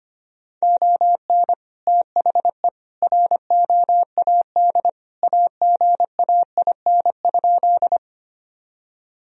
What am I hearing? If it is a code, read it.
Morse code, heard as a keyed tone, about 25 wpm, 700 Hz. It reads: ON THE ROAD AGAIN?
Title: Sentences from Top 1000 Words - Review Part 2 - 25wpm